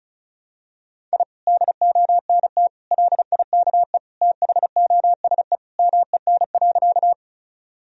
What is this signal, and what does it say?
Morse code, heard as a keyed tone, about 36 wpm, 700 Hz. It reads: I DONT LIKE THOSE MEN.
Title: Sentences from Top 1000 Words - Review Part 1 - 35wpm